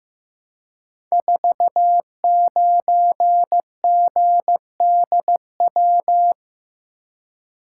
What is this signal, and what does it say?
Morse code, heard as a keyed tone, about 15 wpm, 700 Hz. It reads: 49GDW
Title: Code Groups - 5 Letter:Numbers - 15wpm